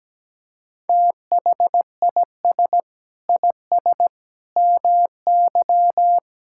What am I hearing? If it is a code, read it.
Morse code, heard as a keyed tone, about 17 wpm, 700 Hz. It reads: THIS IS MY
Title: Sentences from Top 1600 Words 17wpm